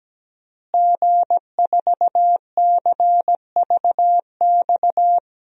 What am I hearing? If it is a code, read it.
Morse code, heard as a keyed tone, about 17 wpm, 700 Hz. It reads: G4CVX
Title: DXCC-100 Call Signs - 5 Characters - Repeated 2x - 17wpm